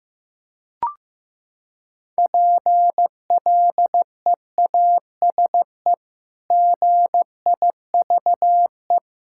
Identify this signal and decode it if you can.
Morse code, heard as a keyed tone, about 16 wpm, 700 Hz. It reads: PLEASE GIVE
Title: Sentences from Top 1600 Words 15wpm